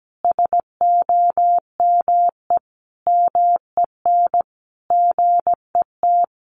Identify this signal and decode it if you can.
Morse code, heard as a keyed tone, about 17 wpm, 700 Hz. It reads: SOME MEN GET
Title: Sentences from Top 900 Words 17wpm